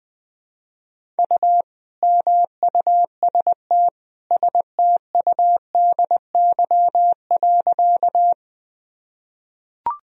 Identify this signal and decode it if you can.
Morse code, heard as a keyed tone, about 21 wpm, 700 Hz. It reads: U MUST STUDY.
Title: Sentences from Top 300 Words and Abbreviations 20wpm